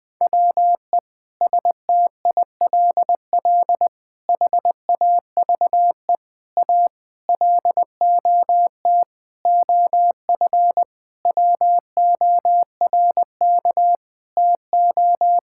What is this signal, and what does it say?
Morse code, heard as a keyed tone, about 20 wpm, 700 Hz. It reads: WE STILL HAVE A LOT OF WORK TO
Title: Sets of 9 Words - 20wpm